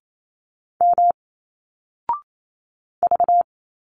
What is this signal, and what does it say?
Morse code, heard as a keyed tone, about 28 wpm, 700 Hz. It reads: M V
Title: Single Letters 28wpm